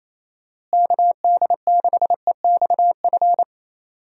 Morse code, heard as a keyed tone, about 28 wpm, 700 Hz. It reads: KD6EXF
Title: USA Call Signs - 6 Characters - Repeated 3x - 28wpm